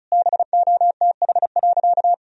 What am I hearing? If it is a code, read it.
Morse code, heard as a keyed tone, about 36 wpm, 700 Hz. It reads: BOTH.
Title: Sentences from Top 1000 Words - Review Part 1 - 35wpm